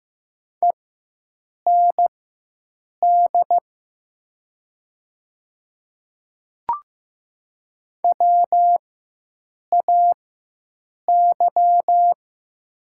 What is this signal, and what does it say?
Morse code, heard as a keyed tone, about 15 wpm, spaced as for 7 wpm, 700 Hz. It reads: END WAY
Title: 4x Character Spacing - Top 200 Words - 15wpm